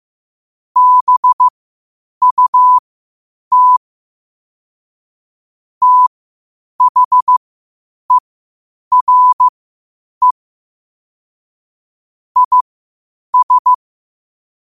Morse code, heard as a keyed tone, about 15 wpm, spaced as for 8 wpm, 1000 Hz. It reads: BUT THERE IS
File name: qotdfiles / 8wpm